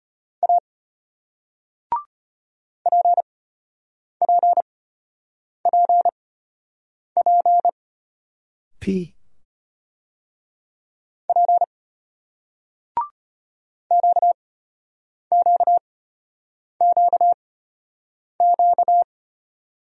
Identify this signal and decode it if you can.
Morse code, heard as a keyed tone, about 34 wpm, spaced as for 8 wpm, 700 Hz. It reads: A PPPP P QQQQ